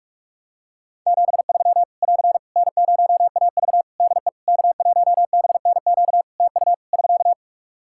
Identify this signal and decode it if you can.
Morse code, heard as a keyed tone, about 43 wpm, 700 Hz. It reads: 73 <AR> N0AV DE K1BNQ TU <SK>
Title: Practice QSO - Exchange 4 - Ending - 45wpm